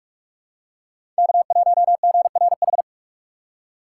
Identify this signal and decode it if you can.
Morse code, heard as a keyed tone, about 44 wpm, 700 Hz. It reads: K1GRH
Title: USA Call Signs - 6x Character Spacing - 45wpm